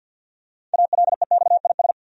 Morse code, heard as a keyed tone, about 51 wpm, 700 Hz. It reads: ALEXIS
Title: USA Common Names - 50wpm